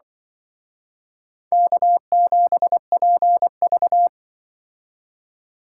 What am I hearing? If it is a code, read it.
Morse code, heard as a keyed tone, about 25 wpm, 700 Hz. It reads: K7PV
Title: Speed Racing - USA Call Signs 20wpm